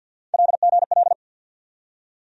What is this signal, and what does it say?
Morse code, heard as a keyed tone, about 53 wpm, 700 Hz. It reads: LGL
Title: USA Call Sign Postfixes - 3 Letters 50wpm